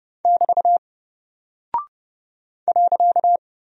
Morse code, heard as a keyed tone, about 30 wpm, 700 Hz. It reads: <BT> .
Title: Single Character (Letter:Number:Punctuation:Prosign) Alt 30wpm